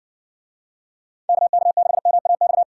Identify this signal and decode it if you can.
Morse code, heard as a keyed tone, about 60 wpm, 700 Hz. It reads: KK6MAX